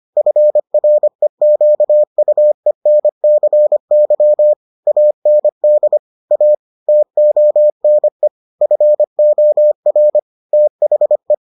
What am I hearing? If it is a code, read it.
Morse code, heard as a keyed tone, about 25 wpm, 600 Hz. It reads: FREQUENCY AND A TONE FOR THE